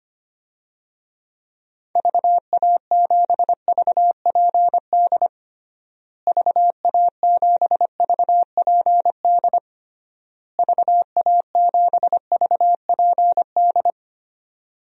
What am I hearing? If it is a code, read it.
Morse code, heard as a keyed tone, about 25 wpm, 700 Hz. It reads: VA7VPD VA7VPD VA7VPD
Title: DXCC-100 Call Signs - 6 Characters - Repeated 3x - 25wpm